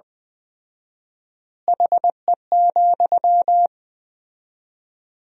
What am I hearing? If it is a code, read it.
Morse code, heard as a keyed tone, about 20 wpm, 700 Hz. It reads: HE,
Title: 4x Word-Spacing - Sentences from Top 500 Words - 20wpm